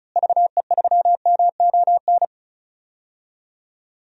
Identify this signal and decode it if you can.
Morse code, heard as a keyed tone, about 34 wpm, 700 Hz. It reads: VE3MON
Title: DXCC-100 Call Signs - 6 Characters - Repeated 3x - 35wpm